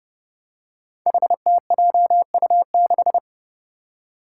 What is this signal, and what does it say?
Morse code, heard as a keyed tone, about 29 wpm, 700 Hz. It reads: HTJU6
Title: Five Challenging Letter-Number Combinations 30wpm